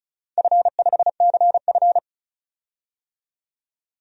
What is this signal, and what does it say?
Morse code, heard as a keyed tone, about 35 wpm, 700 Hz. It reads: F5CF
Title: DXCC-100 Call Signs - 4 Characters - 35wpm